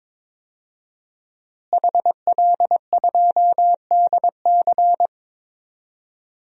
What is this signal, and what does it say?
Morse code, heard as a keyed tone, about 22 wpm, 700 Hz. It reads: HL2DC
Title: DXCC-100 Call Signs - 5 Characters - Repeated 2x - 22wpm